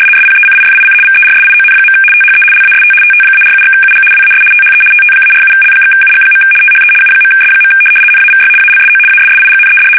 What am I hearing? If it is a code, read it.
RTTY, about 75 baud. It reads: SPECIFIC VERSION. THE LAST TT-47/UG WAS THE TT-47L/UG. THE U.S. NAVY ALSO ASSIGNED
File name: fsk850-75B / teletype-corp